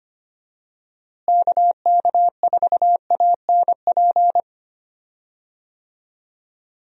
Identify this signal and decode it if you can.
Morse code, heard as a keyed tone, about 25 wpm, 700 Hz. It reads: KK4ANP
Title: USA Call Signs - 6 Characters - Repeated 2x - 25wpm